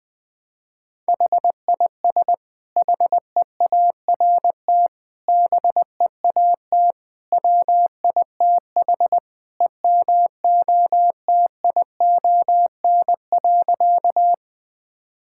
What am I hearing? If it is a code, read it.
Morse code, heard as a keyed tone, about 21 wpm, 700 Hz. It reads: HIS HEART BEAT WITH EMOTION.
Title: Sentences from Top 1600 Words 20wpm